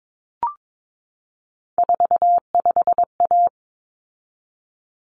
Morse code, heard as a keyed tone, about 22 wpm, 700 Hz. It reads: 45A
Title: Common Cut Numbers - 3-Digit 22wpm